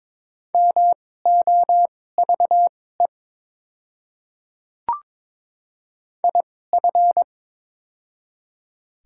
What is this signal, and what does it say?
Morse code, heard as a keyed tone, about 22 wpm, spaced as for 16 wpm, 700 Hz. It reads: MOVE IF
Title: 2x Character Spacing - Top 300 Words - 22wpm